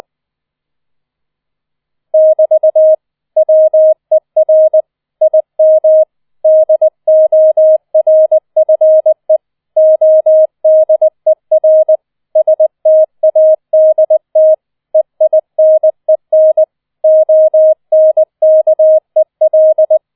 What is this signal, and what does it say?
Morse code, heard as a keyed tone, about 20 wpm, 630 Hz. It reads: <BT> WER IM DORFE ODER STADT EINEN ONKEL